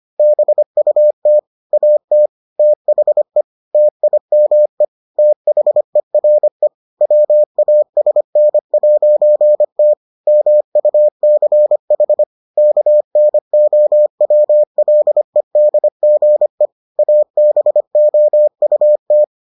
Morse code, heard as a keyed tone, about 25 wpm, 600 Hz. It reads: BUT AT THE TIME THERE WASN'T MUCH KNOWLEDGE ABOUT